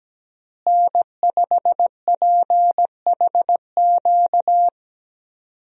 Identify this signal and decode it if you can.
Morse code, heard as a keyed tone, about 17 wpm, 700 Hz. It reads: N5PHQ
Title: USA Call Signs - 5 Characters - Repeated 2x - 17wpm